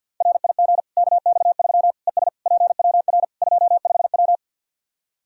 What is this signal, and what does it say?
Morse code, heard as a keyed tone, about 52 wpm, 700 Hz. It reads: RIG KX3 ES PWR 25W
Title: Practice QSO - Exchange 2 - Info - 50wpm